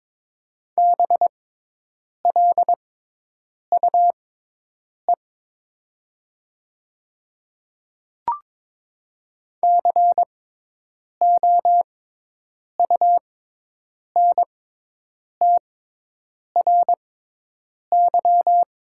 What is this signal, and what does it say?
Morse code, heard as a keyed tone, about 22 wpm, spaced as for 8 wpm, 700 Hz. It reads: BLUE COUNTRY
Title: 6x Character Spacing - Top 500 Words - 22wpm